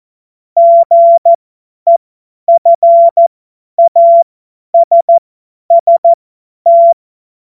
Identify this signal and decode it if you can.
Morse code, heard as a keyed tone, about 14 wpm, spaced as for 10 wpm, 680 Hz. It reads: GEFASST
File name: Morsekurs / Witze / Witz 08. Februar 2022 50 F70